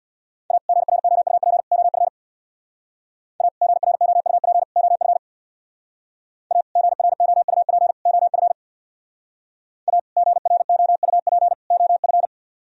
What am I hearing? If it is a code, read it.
Morse code, heard as a keyed tone, about 59 wpm, 700 Hz. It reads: A GROUP OF A GROUP OF A GROUP OF A GROUP OF
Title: Speed-Racing - Sets of 3 Words 50wpm